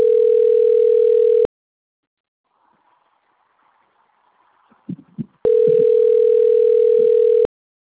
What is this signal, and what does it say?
Ringback tone (North America), 440 + 480 Hz.
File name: rg-1600-unknown- - - .57962